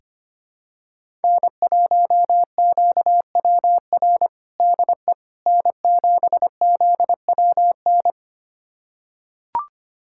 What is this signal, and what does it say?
Morse code, heard as a keyed tone, about 25 wpm, 700 Hz. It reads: N1QWR DE N7ZWN
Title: USA Callsign DE Callsign - 5 Characters - 25wpm